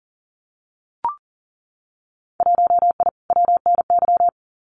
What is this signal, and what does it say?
Morse code, heard as a keyed tone, about 40 wpm, 700 Hz. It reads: 1I WNY
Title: Winter Field Day Exchanges Encore 40wpm